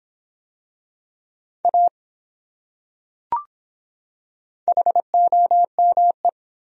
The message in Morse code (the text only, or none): A HOME